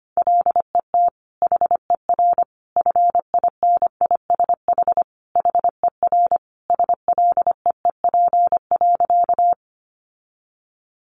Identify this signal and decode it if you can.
Morse code, heard as a keyed tone, about 25 wpm, 700 Hz. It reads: LET HER FINISH HER SLEEP.
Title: Sentences from Top 1200 Words 25wpm